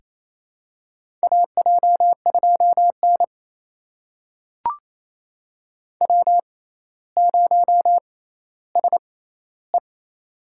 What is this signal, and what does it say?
Morse code, heard as a keyed tone, about 28 wpm, spaced as for 14 wpm, 700 Hz. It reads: AJ2N W0SE
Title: USA Call Signs - 4 Characters - 6x Character Spacing - 28wpm